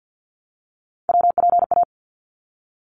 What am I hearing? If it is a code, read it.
Morse code, heard as a keyed tone, about 52 wpm, 700 Hz. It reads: WPA
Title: ARRL Sections 50wpm